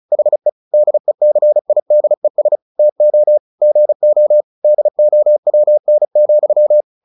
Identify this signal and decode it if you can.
Morse code, heard as a keyed tone, about 34 wpm, 600 Hz. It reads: HE DECIDES TO GO DOWN,
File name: the old man and the sea / the old man and the sea 1 35wpm